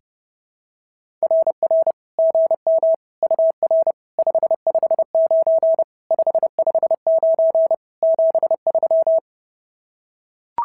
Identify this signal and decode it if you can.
Morse code, heard as a keyed tone, about 30 wpm, 650 Hz. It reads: RR GM UR 559 559 73
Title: SOTA QSO Practice 30wpm